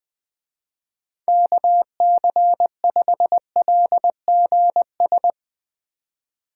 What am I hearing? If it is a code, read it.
Morse code, heard as a keyed tone, about 20 wpm, 700 Hz. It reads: KC5LGS
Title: Speed Racing - USA 2x3 Call Signs 20wpm